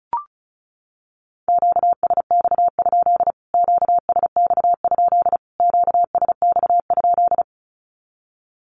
Morse code, heard as a keyed tone, about 34 wpm, 700 Hz. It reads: QSX? QSX? QSX?
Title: Q-Codes Repeated 3x 35wpm